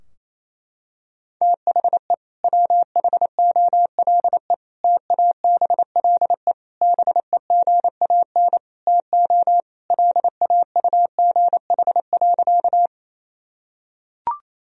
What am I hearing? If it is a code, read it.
Morse code, heard as a keyed tone, about 28 wpm, 700 Hz. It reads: THE WHOLE TABLE BEGAN TO LAUGH.
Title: Sentences from Top 1000 Words - Review Part 2 - 28wpm